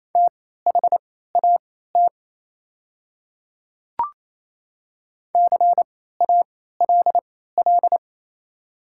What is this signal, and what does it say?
Morse code, heard as a keyed tone, about 27 wpm, spaced as for 16 wpm, 690 Hz. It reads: THAT CALL
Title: 3x Character Spacing - Top 100 Words - 28wpm